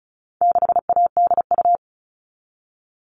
Morse code, heard as a keyed tone, about 34 wpm, 700 Hz. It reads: 6ADU
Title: All Cut Numbers - 4-Digit 35wpm